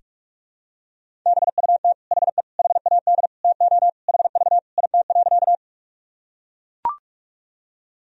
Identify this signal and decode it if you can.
Morse code, heard as a keyed tone, about 42 wpm, 700 Hz. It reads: BUT HE HAD TO HV IT.
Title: Sentences from Top 100 Words and Abbreviations 45wpm